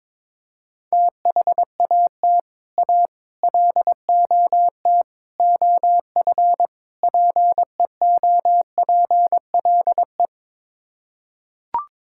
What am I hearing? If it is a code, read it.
Morse code, heard as a keyed tone, about 22 wpm, 700 Hz. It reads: THAT A LOT OF PEOPLE
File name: practice / 22wpm / Build A Phrase - 5-Word Phrases - 22wpm